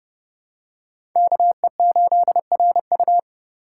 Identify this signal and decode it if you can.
Morse code, heard as a keyed tone, about 30 wpm, 700 Hz. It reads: KE8RU
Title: K1USN SST QSO Practice - 30wpm